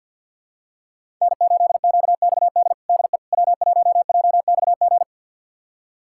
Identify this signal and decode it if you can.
Morse code, heard as a keyed tone, about 48 wpm, 700 Hz. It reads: N8QXD DE W1JXG
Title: USA Callsign DE Callsign 50wpm